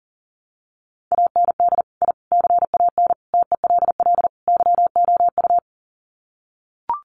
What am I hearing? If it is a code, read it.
Morse code, heard as a keyed tone, about 40 wpm, 700 Hz. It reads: AND I CAN TELL YOU
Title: Sets of 5 Words Encore2 40wpm